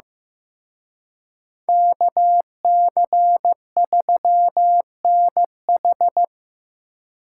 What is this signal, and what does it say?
Morse code, heard as a keyed tone, about 18 wpm, 700 Hz. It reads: KC3NH
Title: Speed Racing - USA 2x2 Call Signs 15wpm